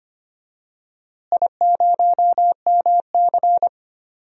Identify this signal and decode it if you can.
Morse code, heard as a keyed tone, about 25 wpm, 700 Hz. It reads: I0MC
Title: DXCC-100 Call Signs - 4 Characters - Repeated 3x - 25wpm